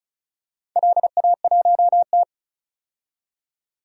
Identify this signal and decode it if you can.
Morse code, heard as a keyed tone, about 36 wpm, 700 Hz. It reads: LA1T
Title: DXCC-100 Call Signs - 4 Characters - 35wpm